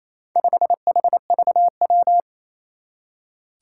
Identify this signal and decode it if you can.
Morse code, heard as a keyed tone, about 28 wpm, 700 Hz. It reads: 5HVW
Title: Four Challenging Letter-Number Combinations 28wpm